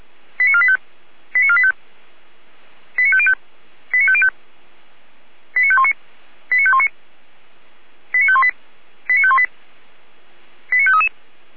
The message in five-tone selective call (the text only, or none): 89485 89485 89505 89505 89519 89519 89518 89518 89533